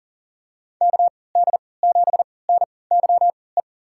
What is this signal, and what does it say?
Morse code, heard as a keyed tone, about 39 wpm, spaced as for 22 wpm, 700 Hz. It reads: KD7NYE
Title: USA Call Signs - 3x Character Spacing - 40wpm